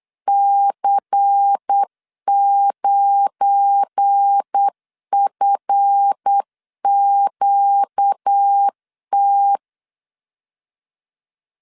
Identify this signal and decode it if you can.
Morse code, heard as a keyed tone, about 8 wpm, 800 Hz. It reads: C9FQT
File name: LEZIONI / 7 GF9Z / 7 GF9 8wpm 3545